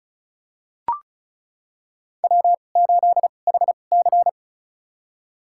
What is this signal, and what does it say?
Morse code, heard as a keyed tone, about 34 wpm, spaced as for 25 wpm, 700 Hz. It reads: W8HC
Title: USA Call Signs - 4 Characters - 2x Character Spacing - 35wpm